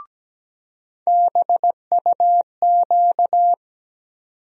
Morse code, heard as a keyed tone, about 17 wpm, 700 Hz. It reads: BUQ
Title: USA Call Sign Postfixes - 3 Letters 17wpm